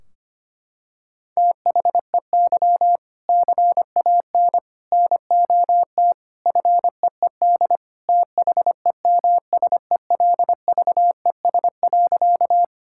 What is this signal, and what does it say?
Morse code, heard as a keyed tone, about 25 wpm, 700 Hz. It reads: THEY CAN NOT FEED THEMSELVES.
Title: Sentences from Top 1600 Words 25wpm